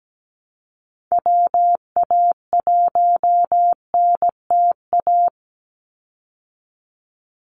Morse code, heard as a keyed tone, about 17 wpm, 700 Hz. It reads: WA1NTA
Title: Call Signs Repeated 3x - 17wpm